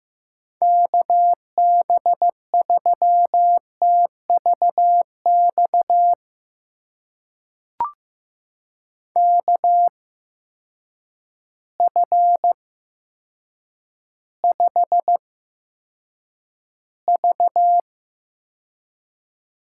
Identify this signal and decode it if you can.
Morse code, heard as a keyed tone, about 15 wpm, spaced as for 7 wpm, 700 Hz. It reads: KB3TVX KF5V
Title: USA Call Signs - 6 Characters - 8x Character Spacing - 15wpm